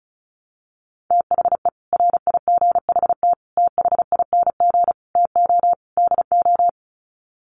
Morse code, heard as a keyed tone, about 34 wpm, 700 Hz. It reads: THE RIGHT THING TO DO
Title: Sets of 5 Words 35wpm